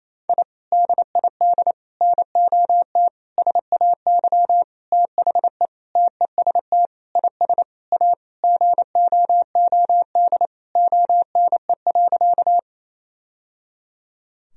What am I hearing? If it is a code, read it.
Morse code, heard as a keyed tone, about 28 wpm, 700 Hz. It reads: I DID NOT SAY THE TEST IS A GOOD ONE.
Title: Sentences from Top 1000 Words - Review Part 3 - 28wpm